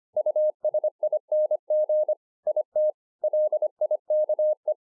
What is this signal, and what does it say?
Morse code, heard as a keyed tone, about 24 wpm, 610 Hz. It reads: USING IT LIKE